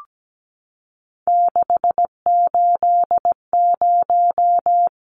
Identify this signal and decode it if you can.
Morse code, heard as a keyed tone, about 17 wpm, 700 Hz. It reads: 680
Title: Numbers - 3-Digit 17wpm